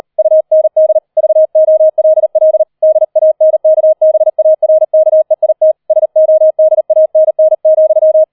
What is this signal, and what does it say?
Morse code, heard as a keyed tone, about 38 wpm, 630 Hz. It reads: UND VOLL DANKBARKEIT SODANN,
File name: 004 Vierter Streich 40wpm